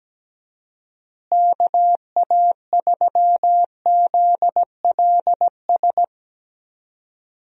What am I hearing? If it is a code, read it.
Morse code, heard as a keyed tone, about 17 wpm, 700 Hz. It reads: KA3ZLS
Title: USA Call Signs - 6 Characters - Repeated 3x - 17wpm